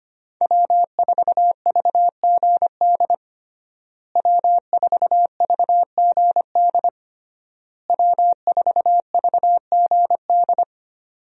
Morse code, heard as a keyed tone, about 25 wpm, 700 Hz. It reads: W4VGD W4VGD W4VGD
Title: USA Call Signs - 5 Characters - Repeated 3x - 25wpm